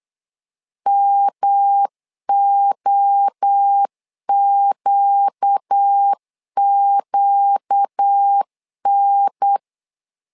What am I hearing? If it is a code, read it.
Morse code, heard as a keyed tone, about 8 wpm, 800 Hz. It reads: MOQQN
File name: LEZIONI / 4 Q 8wpm 2732